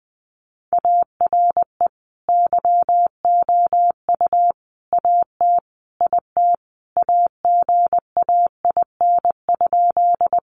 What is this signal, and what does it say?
Morse code, heard as a keyed tone, about 20 wpm, 700 Hz. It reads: ARE YOU AT IT AGAIN?
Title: Sentences from Top 200 Words 20wpm